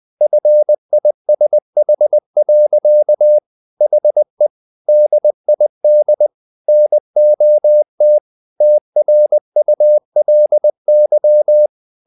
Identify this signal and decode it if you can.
Morse code, heard as a keyed tone, about 20 wpm, 600 Hz. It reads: FISH. HE DID NOT TRULY
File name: the old man and the sea part 2 20wpm